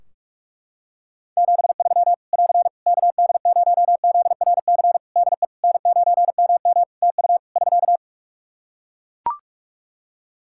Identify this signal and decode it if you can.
Morse code, heard as a keyed tone, about 41 wpm, 700 Hz. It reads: 73 <AR> KD0ZRC DE N9MM TU <SK>
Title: Practice QSO - Exchange 4 - Ending - 45wpm